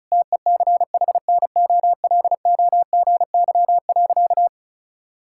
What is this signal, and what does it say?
Morse code, heard as a keyed tone, about 36 wpm, 700 Hz. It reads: TECHNOLOGY.
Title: Sentences from Top 1000 Words - Review Part 5 - 35wpm